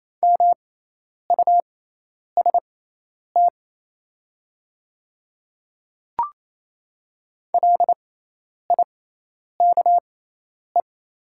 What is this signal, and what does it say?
Morse code, heard as a keyed tone, about 28 wpm, spaced as for 10 wpm, 700 Hz. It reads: MUST LIKE
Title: 6x Character Spacing - Top 200 Words - 28wpm